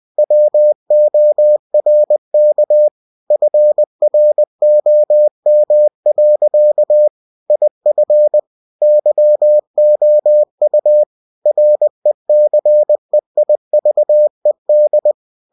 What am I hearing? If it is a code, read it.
Morse code, heard as a keyed tone, about 20 wpm, 600 Hz. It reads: WORK FROM. IF YOU RECEIVED
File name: cw-books / winnie-the-pooh / winnie-20wpm-page-0119